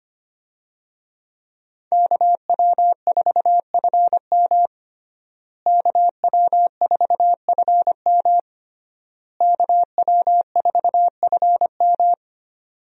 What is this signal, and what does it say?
Morse code, heard as a keyed tone, about 25 wpm, 700 Hz. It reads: KW4FM KW4FM KW4FM
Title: USA Call Signs - 5 Characters - Repeated 3x - 25wpm